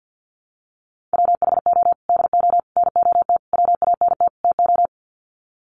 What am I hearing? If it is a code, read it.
Morse code, heard as a keyed tone, about 51 wpm, 700 Hz. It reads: WHO DO NOT WANT TO
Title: Sets of 5 Words Encore2 50wpm